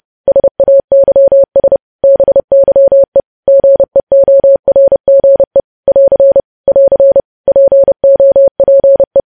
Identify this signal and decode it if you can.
Morse code, heard as a keyed tone, about 30 wpm, 550 Hz. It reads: SAYS BYE GEORGE <AR> <AR> POPE
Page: news headlines in morse code at 30 wpm